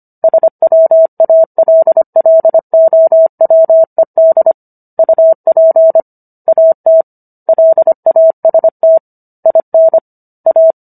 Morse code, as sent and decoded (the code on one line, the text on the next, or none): ... .-- .- .-.. .-.. --- .-- . -.. / ..- .--. / .- - / .-.. .- ... - / .. -. / .-
SWALLOWED UP AT LAST IN A